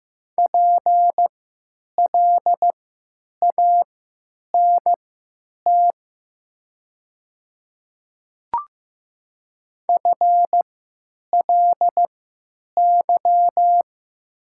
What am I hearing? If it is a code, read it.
Morse code, heard as a keyed tone, about 15 wpm, spaced as for 9 wpm, 700 Hz. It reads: PLANT FLY
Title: 3x Character Spacing - Top 500 Words - 15wpm